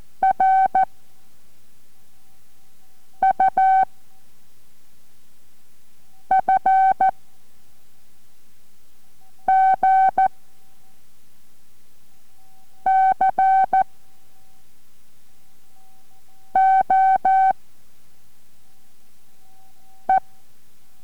Morse code, heard as a keyed tone, about 14 wpm, 760 Hz. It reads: R U F G C O E